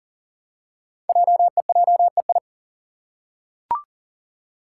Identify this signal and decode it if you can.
Morse code, heard as a keyed tone, about 39 wpm, 690 Hz. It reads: JEJEI
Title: Code Groups - 5 Letter:Numbers - 40wpm